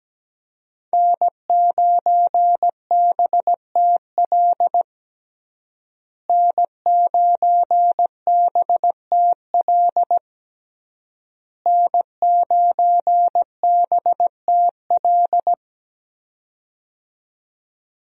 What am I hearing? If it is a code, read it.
Morse code, heard as a keyed tone, about 17 wpm, 700 Hz. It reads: N9BTL N9BTL N9BTL
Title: USA Call Signs - 5 Characters - Repeated 3x - 17wpm